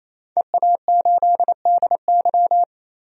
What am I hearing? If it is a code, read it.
Morse code, heard as a keyed tone, about 28 wpm, 700 Hz. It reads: EA8DY
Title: DXCC-100 Call Signs - 5 Characters - 28wpm